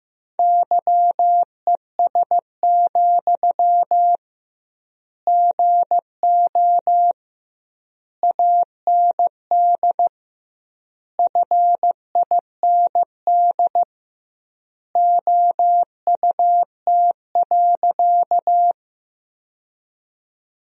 Morse code, heard as a keyed tone, about 15 wpm, 700 Hz. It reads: YES, GO AND FIND OUT.
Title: 2x Word-Spacing - Sentences from Top 500 Words - 15wpm